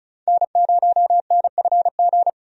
Morse code, heard as a keyed tone, about 34 wpm, 700 Hz. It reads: N0NFG
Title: USA Call Signs - 5 Characters - 35wpm